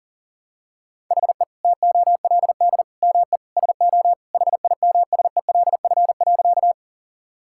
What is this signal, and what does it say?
Morse code, heard as a keyed tone, about 40 wpm, 700 Hz. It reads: HE TOLD ME SO HIMSELF.
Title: Sentences from Top 1600 Words 40wpm